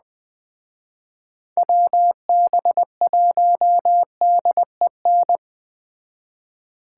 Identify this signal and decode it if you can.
Morse code, heard as a keyed tone, about 21 wpm, 700 Hz. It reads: WB1DEN